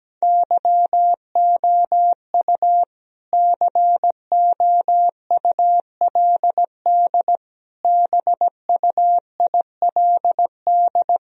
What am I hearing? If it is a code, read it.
Morse code, heard as a keyed tone, about 17 wpm, 700 Hz. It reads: YOU COULD BUILD
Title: Sentences from Top 1000 Words - Review Part 2 - 17wpm